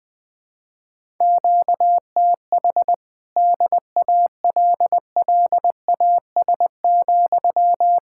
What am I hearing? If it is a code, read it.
Morse code, heard as a keyed tone, about 20 wpm, 700 Hz. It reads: QTH DALLAS,
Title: Speed-Racing Common QSO Phrases Encore - 17wpm